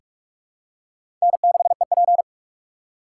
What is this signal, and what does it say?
Morse code, heard as a keyed tone, about 41 wpm, 700 Hz. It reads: N6EP
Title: USA Call Signs - 3x Character Spacing - 45wpm